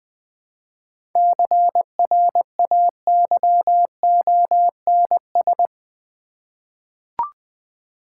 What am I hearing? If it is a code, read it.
Morse code, heard as a keyed tone, about 20 wpm, 700 Hz. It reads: CRAYONS
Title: Easy Words with Hints - 20wpm